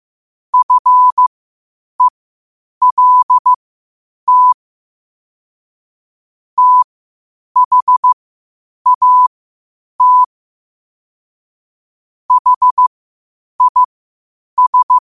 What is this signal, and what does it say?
Morse code, heard as a keyed tone, about 15 wpm, spaced as for 8 wpm, 1000 Hz. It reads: FELT THAT HIS